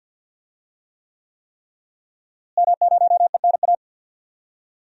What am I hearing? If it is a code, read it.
Morse code, heard as a keyed tone, about 51 wpm, 700 Hz. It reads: M0ENA
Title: DXCC-100 Call Signs - 5 Characters - Repeated 3x - 50wpm